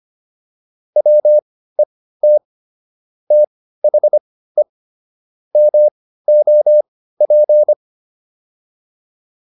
Morse code, heard as a keyed tone, about 25 wpm, spaced as for 15 wpm, 600 Hz. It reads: WET THE MOP